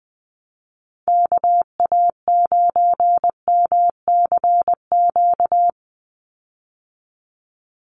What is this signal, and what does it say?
Morse code, heard as a keyed tone, about 19 wpm, 700 Hz. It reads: KA9MCQ
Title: USA Call Signs 20wpm